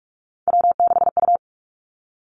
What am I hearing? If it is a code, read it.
Morse code, heard as a keyed tone, about 46 wpm, 700 Hz. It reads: W6U
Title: Three Challenging Letter-Number Combinations 45wpm